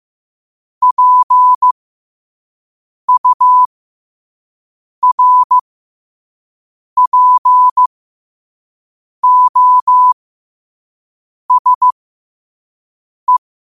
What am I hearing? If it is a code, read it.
Morse code, heard as a keyed tone, about 15 wpm, spaced as for 5 wpm, 1000 Hz. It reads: PURPOSE
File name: qotdfiles / 5wpm